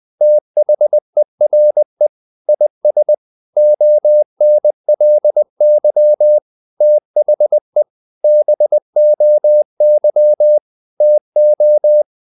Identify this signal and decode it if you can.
Morse code, heard as a keyed tone, about 20 wpm, 600 Hz. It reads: THERE IS ONLY THE BOY TO
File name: the old man and the sea part 2 20wpm